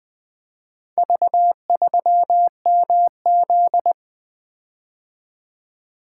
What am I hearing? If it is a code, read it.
Morse code, heard as a keyed tone, about 20 wpm, 700 Hz. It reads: V3MZ
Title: Four Letter-Number Combinations - 20wpm